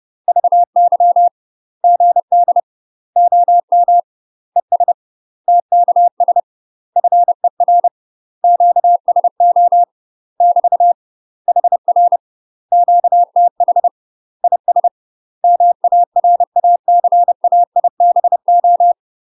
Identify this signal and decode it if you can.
Morse code, heard as a keyed tone, about 30 wpm, 700 Hz. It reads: VY GD OM ES TKS FER QSO <BT> HR QTH IS MARACAIBO
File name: UK MASTER QSOs TRAINING / 30WPM / UK MASTER QSOs 0034